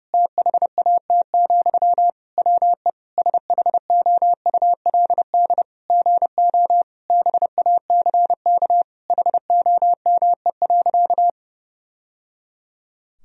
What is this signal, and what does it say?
Morse code, heard as a keyed tone, about 30 wpm, 700 Hz. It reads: THAT, WE SHOULD GO BACK HOME.
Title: Sentences from Top 1000 Words - Review Part 4 - 30wpm